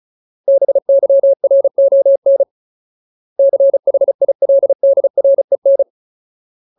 Morse code, heard as a keyed tone, about 34 wpm, 550 Hz. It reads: BYRON CHILDREN
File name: Ultimate Word List / 35WPM Ultimate